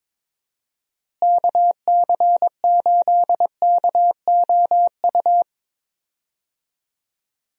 Morse code, heard as a keyed tone, about 21 wpm, 700 Hz. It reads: KC8KOU
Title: USA Call Signs - 6 Characters - Repeated 3x - 22wpm